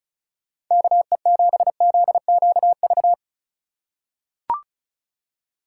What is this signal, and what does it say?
Morse code, heard as a keyed tone, about 36 wpm, 700 Hz. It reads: KE7ZQV
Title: USA Call Signs - 6 Characters - 4x Character Spacing - 35wpm